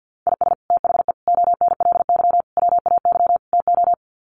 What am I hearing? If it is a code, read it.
Morse code, heard as a keyed tone, about 50 wpm, 700 Hz. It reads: IS THE ONLY WAY TO
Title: Sets of 5 Words Encore 50wpm